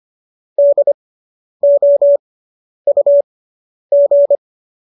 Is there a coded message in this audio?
Yes: Morse code from 0.6 to 4.4 s, heard as a keyed tone, about 25 wpm, spaced as for 10 wpm, 570 Hz. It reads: DOUG